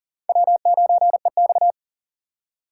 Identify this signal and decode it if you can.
Morse code, heard as a keyed tone, about 52 wpm, 700 Hz. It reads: W9EX